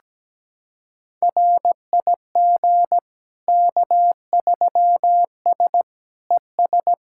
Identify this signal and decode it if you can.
Morse code, heard as a keyed tone, about 17 wpm, 700 Hz. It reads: RIG K3S ES
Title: K1USN SST QSO Practice 17wpm